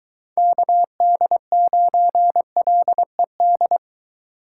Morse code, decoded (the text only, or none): KD9LED